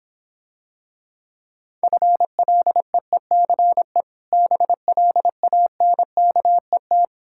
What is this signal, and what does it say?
Morse code, heard as a keyed tone, about 26 wpm, 700 Hz. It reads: FLEECE BLANKET